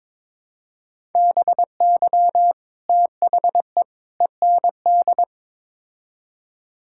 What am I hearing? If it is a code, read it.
Morse code, heard as a keyed tone, about 22 wpm, 700 Hz. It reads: BY THE END
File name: practice / 22wpm / Build A Phrase - 5-Word Phrases - 22wpm